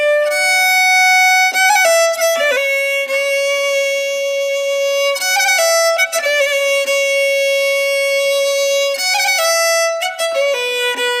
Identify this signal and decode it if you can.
Dial-up modem handshake.